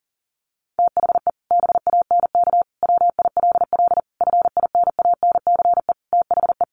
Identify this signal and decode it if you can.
Morse code, heard as a keyed tone, about 40 wpm, 700 Hz. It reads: THE BANK WILL FINANCE THE
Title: Sentences from Top 1500 Words 40wpm